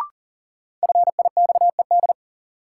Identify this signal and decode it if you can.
Morse code, heard as a keyed tone, about 40 wpm, 700 Hz. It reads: FIXED
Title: Sentences from Top 1100 Words - New Word Drill Alt - 40wpm